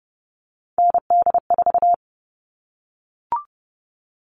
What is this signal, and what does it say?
Morse code, heard as a keyed tone, about 30 wpm, 700 Hz. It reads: ND4
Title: All Cut Numbers - 3-Digit 30wpm